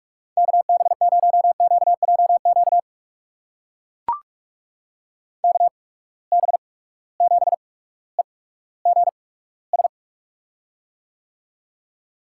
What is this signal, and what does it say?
Morse code, heard as a keyed tone, about 43 wpm, spaced as for 19 wpm, 700 Hz. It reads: KB0QJQ KB7EGS
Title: USA Call Signs - 6 Characters - 8x Character Spacing - 45wpm